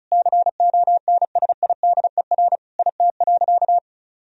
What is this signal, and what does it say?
Morse code, heard as a keyed tone, about 36 wpm, 700 Hz. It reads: CONSIDER IT.
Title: Sentences from Top 1000 Words - Review Part 4 - 35wpm